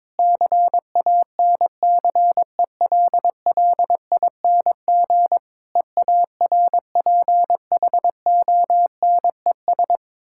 Morse code, heard as a keyed tone, about 22 wpm, 690 Hz. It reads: CANCELLING EARPHONES
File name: Difficult Words with Hints - 22wpm